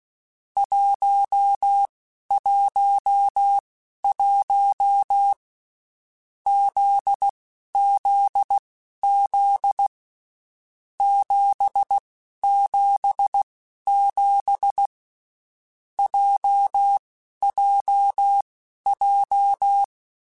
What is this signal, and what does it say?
Morse code, heard as a keyed tone, about 16 wpm, spaced as for 11 wpm, 800 Hz. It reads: 111 ZZZ 777 JJJ
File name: Basic Sets / Basic Letter Groups Collective7 / JZ17@10